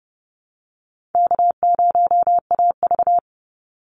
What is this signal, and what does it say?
Morse code, heard as a keyed tone, about 30 wpm, 700 Hz. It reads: K0AV
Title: Easy Call Signs - 30wpm